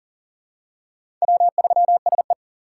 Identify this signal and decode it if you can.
Morse code, heard as a keyed tone, about 40 wpm, 700 Hz. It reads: W3SE
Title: USA Call Signs - 4 Characters - 3x Character Spacing - 40wpm